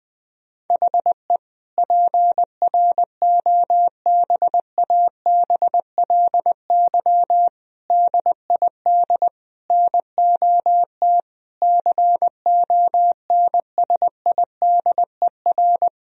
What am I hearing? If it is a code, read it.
Morse code, heard as a keyed tone, about 21 wpm, 700 Hz. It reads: HE PROBABLY DID NOT CONSIDER
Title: Sentences from Top 1000 Words - Review Part 4 - 20wpm